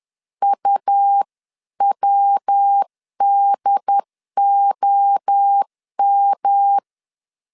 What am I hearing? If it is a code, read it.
Morse code, heard as a keyed tone, about 10 wpm, 800 Hz. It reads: UWDOM
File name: 10wpm